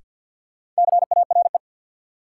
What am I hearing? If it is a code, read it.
Morse code, heard as a keyed tone, about 48 wpm, 700 Hz. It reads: CARE
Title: Sets of 3 Words - New Word Drill - 50wpm